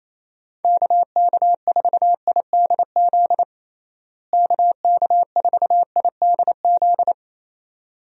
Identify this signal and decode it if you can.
Morse code, heard as a keyed tone, about 28 wpm, 700 Hz. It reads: KK4IDZ KK4IDZ
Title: USA Call Signs - 6 Characters - Repeated 2x - 28wpm